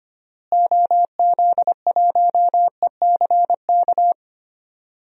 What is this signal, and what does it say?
Morse code, heard as a keyed tone, about 25 wpm, 700 Hz. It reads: OZ1ECK
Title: DXCC-100 Call Signs - 6 Characters - Repeated 3x - 25wpm